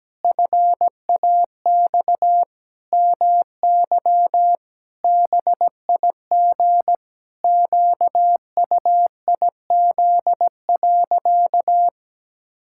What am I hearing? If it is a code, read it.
Morse code, heard as a keyed tone, about 17 wpm, 700 Hz. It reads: FAX MY BIG QUIZ.
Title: Pangram Challenge v2 - 17wpm